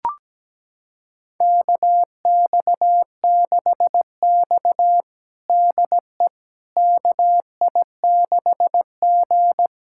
Morse code, heard as a keyed tone, about 17 wpm, 700 Hz. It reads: KX6X DE KI6G
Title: USA Callsign DE Callsign - 4 Characters - 17wpm